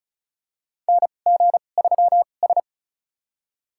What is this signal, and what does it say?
Morse code, heard as a keyed tone, about 36 wpm, spaced as for 25 wpm, 700 Hz. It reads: NG3S
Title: USA Call Signs - 4 Characters - 2x Character Spacing - 35wpm